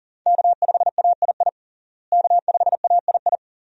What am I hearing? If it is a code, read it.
Morse code, heard as a keyed tone, about 40 wpm, 700 Hz. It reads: K5AII K5AII
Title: USA Call Signs - 5 Characters - Repeated 2x - 40wpm